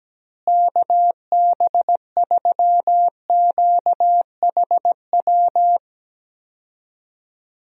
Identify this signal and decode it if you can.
Morse code, heard as a keyed tone, about 17 wpm, 700 Hz. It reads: KB3QHW
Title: USA Call Signs - 6 Characters - Repeated 2x - 17wpm